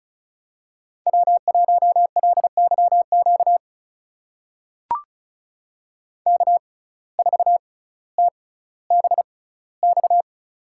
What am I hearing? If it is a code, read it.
Morse code, heard as a keyed tone, about 36 wpm, spaced as for 18 wpm, 700 Hz. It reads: W1LYQ K4TBX
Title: USA Call Signs - 5 Characters - 6x Character Spacing - 35wpm